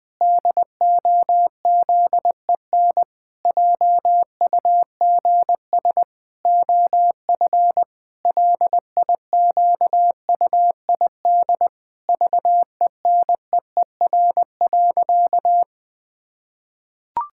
Morse code, heard as a keyed tone, about 20 wpm, 700 Hz. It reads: DOZEN JUGS OF LIQUID VENEER.
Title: Pangram Challenge v2 - 20wpm